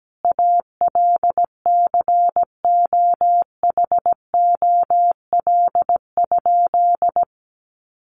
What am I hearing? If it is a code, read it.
Morse code, heard as a keyed tone, about 17 wpm, 700 Hz. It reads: ALCOHOL?
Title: Sentences from Top 1400 Words 17wpm